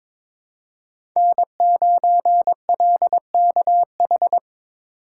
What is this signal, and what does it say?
Morse code, heard as a keyed tone, about 22 wpm, 700 Hz. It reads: N9LKH
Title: USA Call Signs - 8x Character Spacing - 22wpm